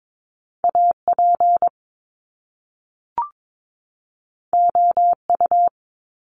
Morse code, heard as a keyed tone, about 22 wpm, 700 Hz. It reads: AP OU
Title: Top Two Letter Combinations in Words 22wpm